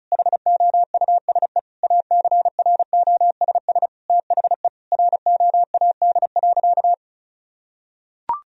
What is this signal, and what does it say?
Morse code, heard as a keyed tone, about 34 wpm, 700 Hz. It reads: HOUSE ACROSS THE ROAD.
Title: Sentences from Top 1000 Words - Review Part 1 - 35wpm